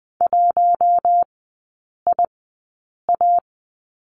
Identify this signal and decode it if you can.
Morse code, heard as a keyed tone, about 20 wpm, spaced as for 8 wpm, 700 Hz. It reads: 1IA
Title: 067 Lesson 12e - Letter d - Three Characters 20wpm